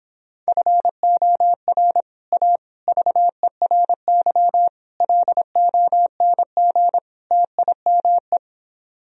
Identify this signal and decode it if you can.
Morse code, heard as a keyed tone, about 29 wpm, 700 Hz. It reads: FOR A VERY LONG TIME